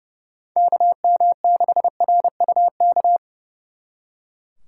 Morse code, heard as a keyed tone, about 37 wpm, 700 Hz. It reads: KM6RUK
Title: Speed Racing - USA Call Signs 30wpm